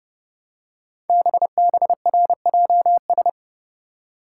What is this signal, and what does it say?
Morse code, heard as a keyed tone, about 30 wpm, 700 Hz. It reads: BBRJS
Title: Code Groups - 5 Letters - 30wpm